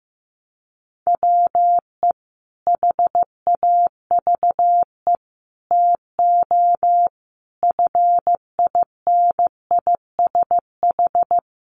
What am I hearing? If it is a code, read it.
Morse code, heard as a keyed tone, about 15 wpm, 700 Hz. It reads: WE HAVE TO FINISH
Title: Sentences from Top 1200 Words 15wpm